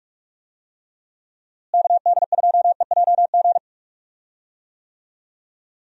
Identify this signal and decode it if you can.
Morse code, heard as a keyed tone, about 42 wpm, 700 Hz. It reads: KD2EJG
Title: USA Call Signs - 6 Characters - Repeated 3x - 45wpm